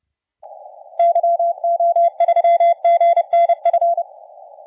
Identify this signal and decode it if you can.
Morse code, heard as a keyed tone, about 29 wpm, 680 Hz. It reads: YO3GNF
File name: la4c / sac-cw-2017 / CONTEST AUDIO 1100Z